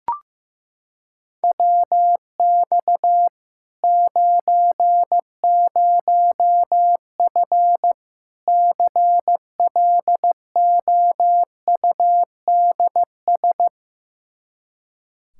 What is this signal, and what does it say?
Morse code, heard as a keyed tone, about 15 wpm, 700 Hz. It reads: WX 90F CLOUDS
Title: Practice QSO - Exchange 2 - Info - 15wpm